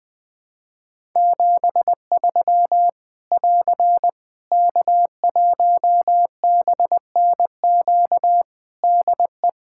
Morse code, heard as a keyed tone, about 20 wpm, 700 Hz. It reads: 73 <AR> K1BNQ DE
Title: Practice QSO - Exchange 4 - Ending - 20wpm